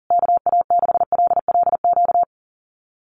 Morse code, heard as a keyed tone, about 37 wpm, 700 Hz. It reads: KA6LLQ
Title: USA Call Signs Repeated 3x 40wpm